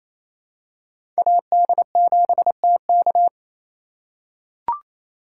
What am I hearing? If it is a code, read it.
Morse code, heard as a keyed tone, about 28 wpm, 700 Hz. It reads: AD7TK
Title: USA Call Signs - 5 Characters - 4x Character Spacing - 28wpm